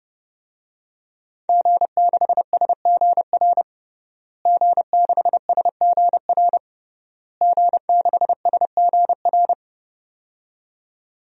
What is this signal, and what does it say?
Morse code, heard as a keyed tone, about 30 wpm, 700 Hz. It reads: G6SGR G6SGR G6SGR
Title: DXCC-100 Call Signs - 5 Characters - Repeated 3x - 30wpm